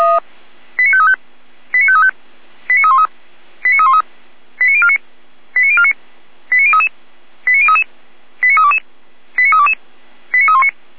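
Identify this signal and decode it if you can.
Five-tone selective call (ZVEI), heard as quick runs of five tones, tones of about 70 ms. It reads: 89535 89535 89313 89313 89049 89049 89033 89033 89322 89322 89319
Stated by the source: ZVEI five-tone selective call received on SDR.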